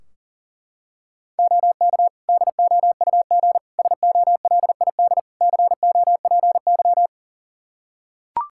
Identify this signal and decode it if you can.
Morse code, heard as a keyed tone, about 40 wpm, 700 Hz. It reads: OK DOUG SOLID COPY
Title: K1USN SST QSO Practice 40wpm